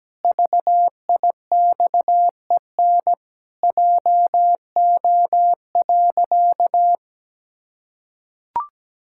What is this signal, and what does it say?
Morse code, heard as a keyed tone, about 17 wpm, 700 Hz. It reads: VIXEN JO.
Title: Pangram Challenge v2 - 17wpm